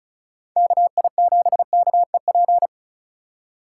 Morse code, heard as a keyed tone, about 36 wpm, 700 Hz. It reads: KI7KEP
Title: USA Call Signs - 6 Characters - 8x Character Spacing - 35wpm